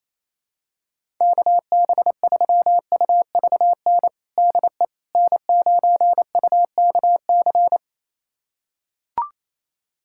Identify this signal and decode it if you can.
Morse code, heard as a keyed tone, about 28 wpm, 700 Hz. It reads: KB3UVN DE N9UKC
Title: USA Callsign DE Callsign 28wpm